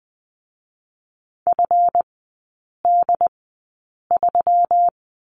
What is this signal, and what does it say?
Morse code, heard as a keyed tone, about 20 wpm, spaced as for 8 wpm, 700 Hz. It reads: FD3
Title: 145 Lesson 24e - Letter y - Three Characters 20wpm